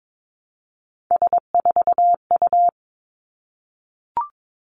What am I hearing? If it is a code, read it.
Morse code, heard as a keyed tone, about 22 wpm, 700 Hz. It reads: S4U
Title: Three Challenging Letter-Number Combinations 22wpm